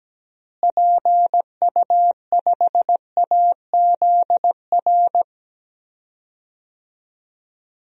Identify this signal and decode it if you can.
Morse code, heard as a keyed tone, about 17 wpm, 700 Hz. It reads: PU5AZR
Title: DXCC-100 Call Signs - 6 Characters - Repeated 3x - 17wpm